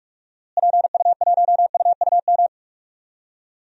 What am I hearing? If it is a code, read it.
Morse code, heard as a keyed tone, about 45 wpm, 700 Hz. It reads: PU1UUM
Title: DXCC-100 Call Signs - 6 Characters - Repeated 2x - 45wpm